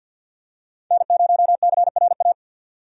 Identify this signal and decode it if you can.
Morse code, heard as a keyed tone, about 52 wpm, 700 Hz. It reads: N0CRA
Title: USA Call Signs - 5 Characters - 50wpm